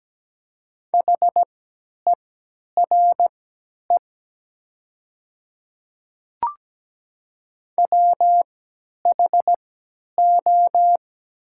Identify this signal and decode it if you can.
Morse code, heard as a keyed tone, about 17 wpm, spaced as for 10 wpm, 700 Hz. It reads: HERE WHO
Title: 3x Character Spacing - Top 200 Words - 17wpm